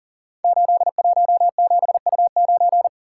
Morse code, heard as a keyed tone, about 42 wpm, 690 Hz. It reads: 817U9
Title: Code Groups - 5 Letter:Numbers - 40wpm